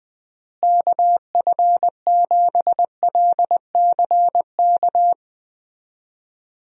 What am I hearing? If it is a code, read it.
Morse code, heard as a keyed tone, about 20 wpm, 700 Hz. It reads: KF7LCK
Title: USA Call Signs - 6 Characters - Repeated 3x - 20wpm